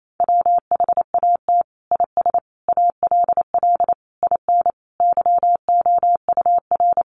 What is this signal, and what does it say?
Morse code, heard as a keyed tone, about 28 wpm, 700 Hz. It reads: WHAT IS ALL IN YOUR
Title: Sentences from Top 1200 Words 28wpm